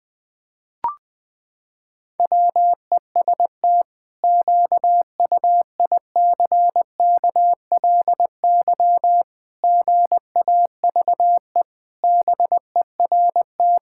Morse code, heard as a keyed tone, about 20 wpm, 700 Hz. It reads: WEST QUICKLY GAVE BERT
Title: Pangram Challenge v2 - 20wpm